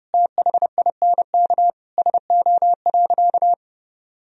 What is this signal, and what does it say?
Morse code, heard as a keyed tone, about 30 wpm, 700 Hz. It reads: THINK SO.
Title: Sentences from Top 1000 Words - Review Part 1 - 30wpm